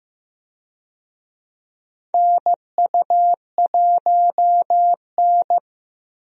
Morse code, heard as a keyed tone, about 15 wpm, 700 Hz. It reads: NU1N
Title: USA Call Signs - 4 Characters - Repeated 3x - 15wpm